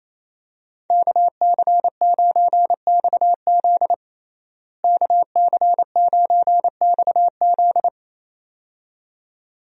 Morse code, heard as a keyed tone, about 28 wpm, 700 Hz. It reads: KC9XZ KC9XZ
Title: USA Call Signs - 5 Characters - Repeated 2x - 28wpm